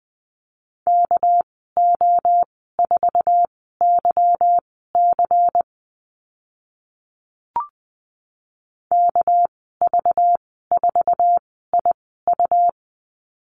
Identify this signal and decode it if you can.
Morse code, heard as a keyed tone, about 20 wpm, spaced as for 15 wpm, 700 Hz. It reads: KO4YC KV4IU
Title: 207 Lesson 33g - Letter j - Call Signs 20wpm